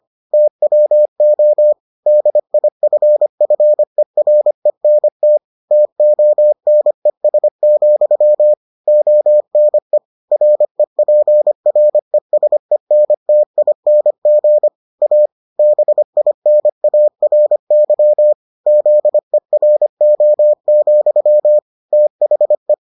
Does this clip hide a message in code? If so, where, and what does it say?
Yes: Morse code from 0.3 to 22.7 s, heard as a keyed tone, about 25 wpm, 600 Hz. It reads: TWO DIFFERENT TONES, ONE REPRESENTING A BINARY ZERO, THE